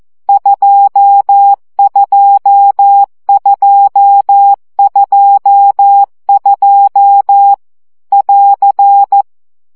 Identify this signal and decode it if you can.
Morse code, heard as a keyed tone, about 14 wpm, 800 Hz. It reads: 22222 <AR>
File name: CW gyakorlas 2 12 baud